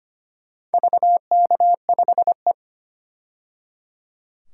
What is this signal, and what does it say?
Morse code, heard as a keyed tone, about 25 wpm, 700 Hz. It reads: VK5E
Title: DXCC-100 Call Signs - 4 Characters - 25wpm